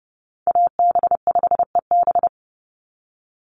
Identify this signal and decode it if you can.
Morse code, heard as a keyed tone, about 30 wpm, 700 Hz. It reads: AB5EB
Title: Call Signs - 30wpm